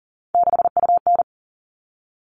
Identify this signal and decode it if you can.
Morse code, heard as a keyed tone, about 40 wpm, 700 Hz. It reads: 6UN
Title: All Cut Numbers - 3-Digit 40wpm